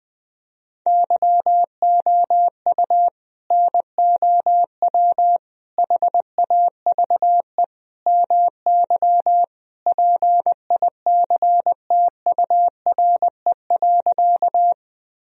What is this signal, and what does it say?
Morse code, heard as a keyed tone, about 20 wpm, 700 Hz. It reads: YOU NOW HAVE MY PICTURE.
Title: Sentences from Top 1000 Words - Review Part 1 - 20wpm